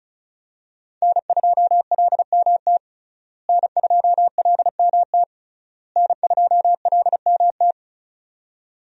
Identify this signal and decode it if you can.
Morse code, heard as a keyed tone, about 36 wpm, 700 Hz. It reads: N2LMT N2LMT N2LMT
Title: USA Call Signs - 5 Characters - Repeated 3x - 35wpm